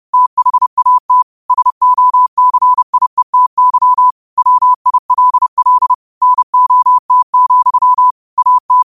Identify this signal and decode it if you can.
Morse code, heard as a keyed tone, about 30 wpm, 1000 Hz. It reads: THAT SOCIETY WILL NOT, AT